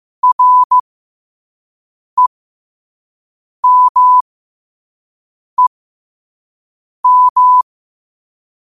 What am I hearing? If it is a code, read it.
Morse code, heard as a keyed tone, about 15 wpm, spaced as for 5 wpm, 1000 Hz. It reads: REMEM
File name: qotdfiles / 5wpm